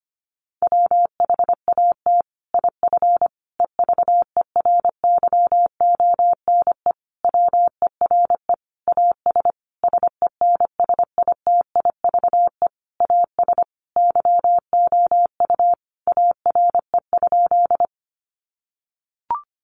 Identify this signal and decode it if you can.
Morse code, heard as a keyed tone, about 25 wpm, 700 Hz. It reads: WHAT IF EVERYONE WERE AS SENSITIVE AS YOU ARE?
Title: Sentences from Top 1400 Words 25wpm